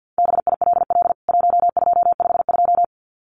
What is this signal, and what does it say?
Morse code, heard as a keyed tone, about 49 wpm, 700 Hz. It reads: BILL 1252
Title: CWT Exchange 50wpm